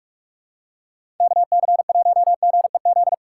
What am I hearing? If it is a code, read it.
Morse code, heard as a keyed tone, about 46 wpm, 700 Hz. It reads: KC1GEZ
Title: USA Call Signs - 2x Character Spacing - 45wpm